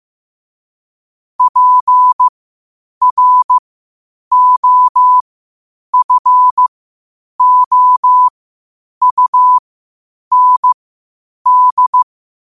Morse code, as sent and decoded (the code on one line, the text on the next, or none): .--. .-. --- ..-. --- ..- -. -..
PROFOUND